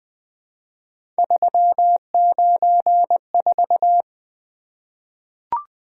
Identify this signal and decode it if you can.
Morse code, heard as a keyed tone, about 20 wpm, 700 Hz. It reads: 394
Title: Three Letter-Number Combinations - 20wpm